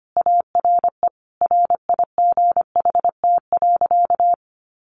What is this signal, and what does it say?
Morse code, heard as a keyed tone, about 25 wpm, 700 Hz. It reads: ARE RIGHT.
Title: Sentences from Top 300 Words 25wpm